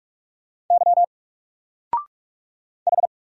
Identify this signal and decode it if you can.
Morse code, heard as a keyed tone, about 44 wpm, 700 Hz. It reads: Y H
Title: Single Character (Letter:Number:Punctuation:Prosign) 45wpm